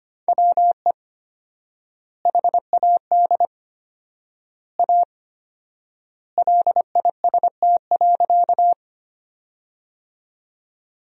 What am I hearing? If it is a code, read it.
Morse code, heard as a keyed tone, about 25 wpm, 700 Hz. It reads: WE HAD A LIST.
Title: 4x Word-Spacing - Sentences from Top 400 Words - 25wpm